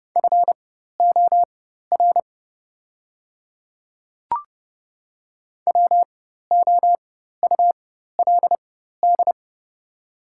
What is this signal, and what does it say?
Morse code, heard as a keyed tone, about 30 wpm, spaced as for 14 wpm, 700 Hz. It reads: FOR WOULD
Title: 4x Character Spacing - Top 200 Words - 30wpm